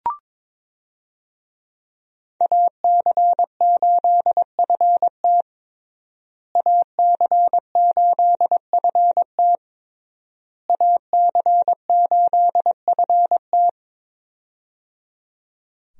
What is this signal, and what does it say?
Morse code, heard as a keyed tone, about 22 wpm, 700 Hz. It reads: AC8FT AC8FT AC8FT
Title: USA Call Signs - 5 Characters - Repeated 3x - 22wpm